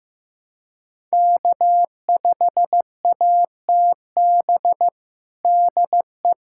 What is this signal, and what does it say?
Morse code, heard as a keyed tone, about 15 wpm, 700 Hz. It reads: K5ATB DE
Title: USA Callsign DE Callsign 15wpm